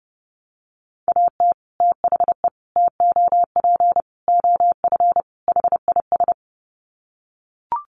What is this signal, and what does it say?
Morse code, heard as a keyed tone, about 30 wpm, 700 Hz. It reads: AT THE TOP OF HIS
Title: Sets of 5 Words Encore 30wpm